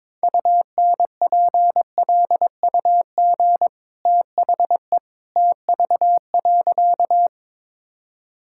Morse code, heard as a keyed tone, about 22 wpm, 690 Hz. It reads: UNPLUG THE TV.
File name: practice / 22wpm / Easy Sentences with Hints - 22wpm